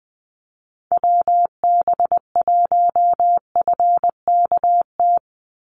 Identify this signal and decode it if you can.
Morse code, heard as a keyed tone, about 20 wpm, 700 Hz. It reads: WB1FKT
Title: USA Call Signs 20wpm